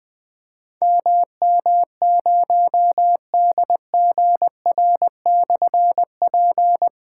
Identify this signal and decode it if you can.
Morse code, heard as a keyed tone, about 20 wpm, 700 Hz. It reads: MM0DGR/P
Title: Difficult Call Signs 20wpm